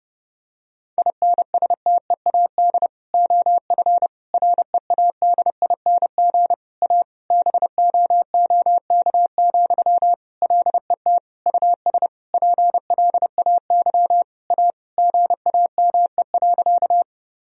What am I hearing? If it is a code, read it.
Morse code, heard as a keyed tone, about 30 wpm, 700 Hz. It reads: INSTEAD OF READING A BOOK, LET US PLAY A GAME.
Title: Sentences from Top 1000 Words - Review Part 4 - 30wpm